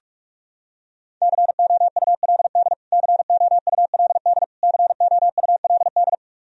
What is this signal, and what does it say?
Morse code, heard as a keyed tone, about 46 wpm, 700 Hz. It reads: COULD COULD COULD
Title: Top 100 Words Repeated 45wpm